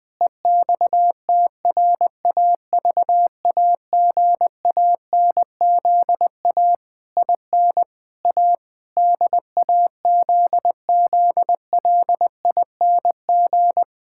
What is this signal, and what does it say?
Morse code, heard as a keyed tone, about 20 wpm, 700 Hz. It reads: EXTRAVAGANZA IN A DAZZLING
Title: Pangram Challenge v2 - 20wpm